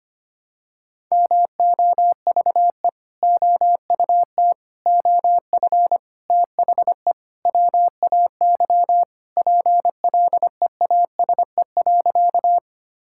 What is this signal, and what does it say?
Morse code, heard as a keyed tone, about 25 wpm, 700 Hz. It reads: MOVE OUT OF THE WAY PLEASE.
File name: practice / 25wpm / Sentences with 4-Word Phrases - 25wpm